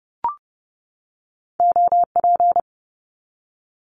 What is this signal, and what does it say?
Morse code, heard as a keyed tone, about 31 wpm, 700 Hz. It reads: OP
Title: Abbreviations 30wpm